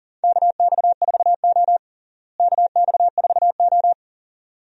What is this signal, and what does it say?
Morse code, heard as a keyed tone, about 40 wpm, 700 Hz. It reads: KX4O KX4O
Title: USA Call Signs - 4 Characters - Repeated 2x - 40wpm